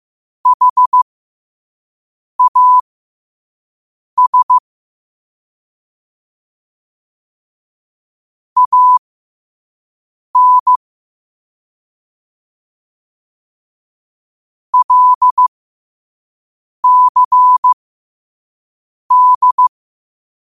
Morse code, heard as a keyed tone, about 15 wpm, spaced as for 5 wpm, 1000 Hz. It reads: HAS AN LCD
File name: qotdfiles / 5wpm